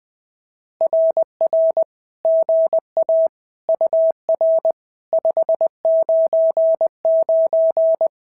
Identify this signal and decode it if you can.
Morse code, heard as a keyed tone, about 20 wpm, 650 Hz. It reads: RR GA UR 599
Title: SOTA QSO Practice 20wpm